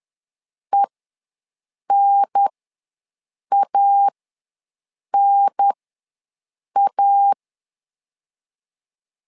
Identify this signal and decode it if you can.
Morse code, heard as a keyed tone, about 10 wpm, spaced as for 6 wpm, 800 Hz. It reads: ENANA
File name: 2 UHDN ws 10wpm